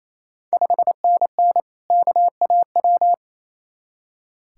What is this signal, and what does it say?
Morse code, heard as a keyed tone, about 28 wpm, 700 Hz. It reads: 5NN KAW
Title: Ontario QSO Party Exchange 28wpm